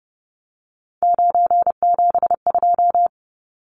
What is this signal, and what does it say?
Morse code, heard as a keyed tone, about 30 wpm, 700 Hz. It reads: 972
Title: Numbers - 3-Digit 30wpm